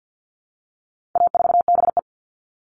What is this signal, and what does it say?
Morse code, heard as a keyed tone, about 50 wpm, 700 Hz. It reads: A4BE
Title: All Cut Numbers - 4-Digit 50wpm